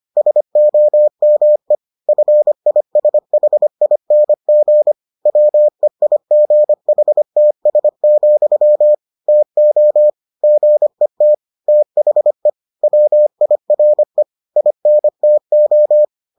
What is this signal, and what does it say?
Morse code, heard as a keyed tone, about 25 wpm, 600 Hz. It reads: SOME FISHING WEIGHTS, TO GET THE WIRE INTO